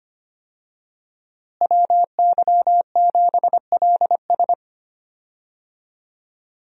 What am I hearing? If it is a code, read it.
Morse code, heard as a keyed tone, about 25 wpm, 700 Hz. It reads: WY7LS
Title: USA Call Signs - 5 Characters - Repeated 2x - 25wpm